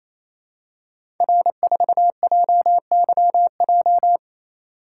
Morse code, heard as a keyed tone, about 28 wpm, 700 Hz. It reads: R4JYJ
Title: Code Groups - 5 Letter:Numbers - 28wpm